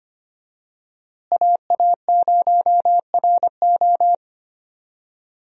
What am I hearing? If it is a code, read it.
Morse code, heard as a keyed tone, about 25 wpm, 700 Hz. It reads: AA0RO
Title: USA Call Signs - 5 Characters - Repeated 3x - 25wpm